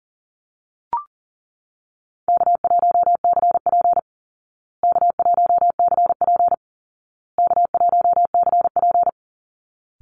Morse code, heard as a keyed tone, about 40 wpm, 700 Hz. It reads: K1CP K1CP K1CP
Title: Call Signs Repeated 2x - 40wpm